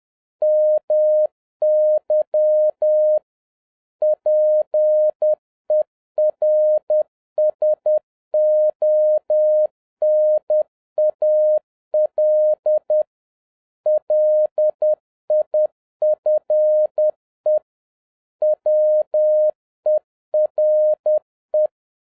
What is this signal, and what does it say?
Morse code, heard as a keyed tone, about 10 wpm, 610 Hz. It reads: MY PERSONAL LIFE WERE